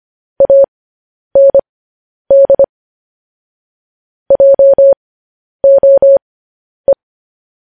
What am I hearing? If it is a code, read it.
Morse code, heard as a keyed tone, about 25 wpm, spaced as for 10 wpm, 550 Hz. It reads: AND JOE
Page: news-headlines-in-morse-code-at-10-wpm